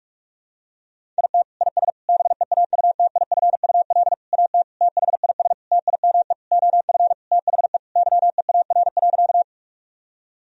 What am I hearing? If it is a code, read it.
Morse code, heard as a keyed tone, about 46 wpm, 700 Hz. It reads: IT IS BEAUTIFUL AT THIS TIME OF THE YEAR.
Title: Sentences from Top 1000 Words - Review Part 1 - 45wpm